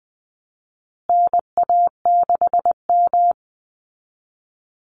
Morse code, heard as a keyed tone, about 20 wpm, 700 Hz. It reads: NA6M
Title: Call Signs Repeated 3x - 20wpm